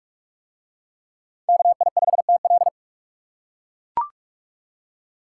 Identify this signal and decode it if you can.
Morse code, heard as a keyed tone, about 41 wpm, 700 Hz. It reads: KI5TL